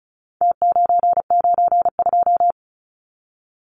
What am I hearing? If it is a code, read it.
Morse code, heard as a keyed tone, about 34 wpm, 700 Hz. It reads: T992
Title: Numbers - Contest Style - 4-Digit 35wpm